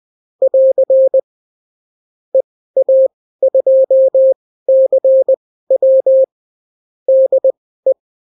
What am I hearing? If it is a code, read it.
Morse code, heard as a keyed tone, about 20 wpm, spaced as for 14 wpm, 530 Hz. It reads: <AR> EA2CW DE
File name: CW QSO libroCW4 20-10ppm